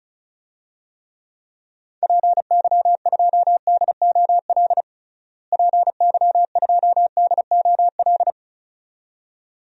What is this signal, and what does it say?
Morse code, heard as a keyed tone, about 34 wpm, 700 Hz. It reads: PY2DOL PY2DOL
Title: DXCC-100 Call Signs - 6 Characters - Repeated 2x - 35wpm